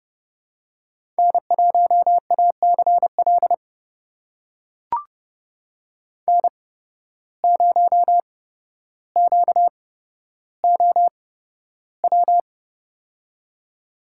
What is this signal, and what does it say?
Morse code, heard as a keyed tone, about 31 wpm, spaced as for 13 wpm, 700 Hz. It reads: N1ACL N0QOW
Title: USA Call Signs - 5 Characters - 8x Character Spacing - 30wpm